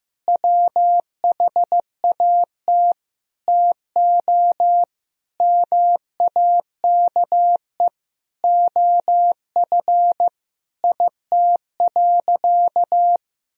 Morse code, heard as a keyed tone, about 15 wpm, 700 Hz. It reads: WHAT TO MAKE OF IT.
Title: Sentences from Top 1000 Words - Review Part 1 - 15wpm